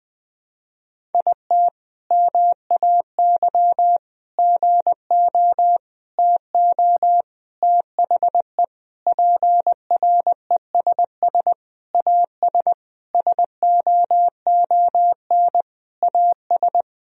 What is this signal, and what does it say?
Morse code, heard as a keyed tone, about 20 wpm, 700 Hz. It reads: IT MAY GO TO THE PRESS AS SOON AS
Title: Sentences from Top 1000 Words - Review Part 1 - 20wpm